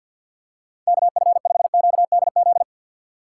Morse code, heard as a keyed tone, about 47 wpm, 700 Hz. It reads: KF5QDZ
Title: USA Call Signs - 6 Characters - 3x Character Spacing - 50wpm